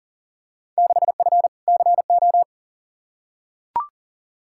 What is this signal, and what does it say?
Morse code, heard as a keyed tone, about 40 wpm, 700 Hz. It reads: 6F CO
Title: Field Day Exchanges Encore - 40wpm